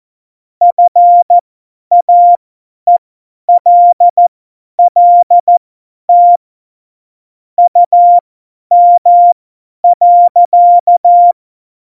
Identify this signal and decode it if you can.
Morse code, heard as a keyed tone, about 14 wpm, spaced as for 10 wpm, 700 Hz. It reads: FAELLT UM.
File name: Morsekurs / Witze / Witz 04 Juli 17, Tempo50 F70